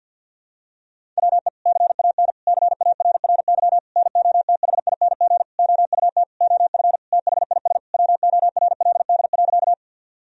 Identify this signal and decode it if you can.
Morse code, heard as a keyed tone, about 50 wpm, 700 Hz. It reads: WE CAN CARRY NOTHING OUT OF THIS WORLD.
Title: Sentences from Top 1000 Words - Review Part 1 - 50wpm